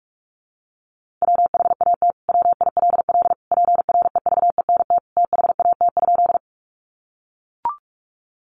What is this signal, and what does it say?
Morse code, heard as a keyed tone, about 44 wpm, 700 Hz. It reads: WHAT WILL PREVENT THAT?
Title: Sentences from Top 1100 Words 45wpm